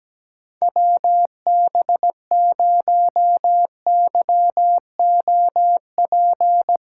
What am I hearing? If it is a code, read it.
Morse code, heard as a keyed tone, about 17 wpm, 700 Hz. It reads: WB0YOP
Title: USA Callsign DE Callsign 17wpm